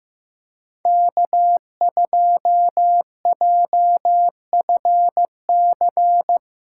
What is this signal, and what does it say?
Morse code, heard as a keyed tone, about 15 wpm, 700 Hz. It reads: K2JFC
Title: USA Call Signs - 5 Characters - 8x Character Spacing - 15wpm